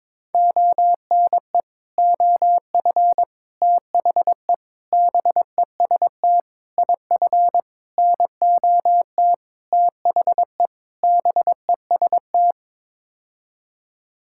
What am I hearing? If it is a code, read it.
Morse code, heard as a keyed tone, about 22 wpm, 700 Hz. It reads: ONE OF THE BEST IF NOT THE BEST
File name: practice / 22wpm / Sets of 8 Words - 22wpm